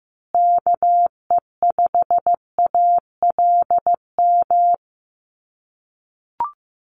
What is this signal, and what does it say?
Morse code, heard as a keyed tone, about 15 wpm, 700 Hz. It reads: KE5ALM
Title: USA Call Signs Repeated 3x 15wpm